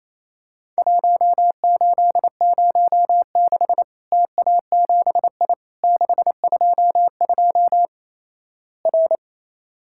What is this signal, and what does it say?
Morse code, heard as a keyed tone, about 27 wpm, 700 Hz. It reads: 1806 TA7I 622 R
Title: Worked All Europe DX Contest - QTC Exchange - 28wpm